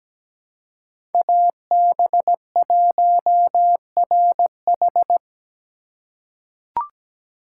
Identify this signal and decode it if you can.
Morse code, heard as a keyed tone, about 17 wpm, 700 Hz. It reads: AB1RH
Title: USA Call Signs - 5 Characters - 3x Character Spacing - 17wpm